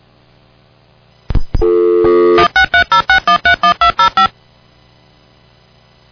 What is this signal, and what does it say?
DTMF, heard as telephone touch tones, tones of about 110 ms. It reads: 16306131205
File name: Dial-Up-Modem